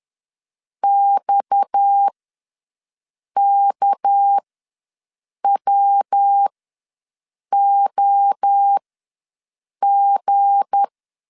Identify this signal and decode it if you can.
Morse code, heard as a keyed tone, about 10 wpm, spaced as for 6 wpm, 800 Hz. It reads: XKWOG
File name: LEZIONI / 7 GF9Z / 7 G ws 10wpm 3843